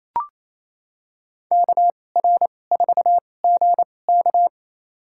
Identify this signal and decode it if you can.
Morse code, heard as a keyed tone, about 28 wpm, spaced as for 20 wpm, 700 Hz. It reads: KR4GK
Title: USA Call Signs - 2x Character Spacing - 28wpm